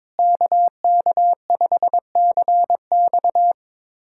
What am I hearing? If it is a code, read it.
Morse code, heard as a keyed tone, about 21 wpm, 700 Hz. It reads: KK5CX
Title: USA Callsign DE Callsign - 5 Characters - 22wpm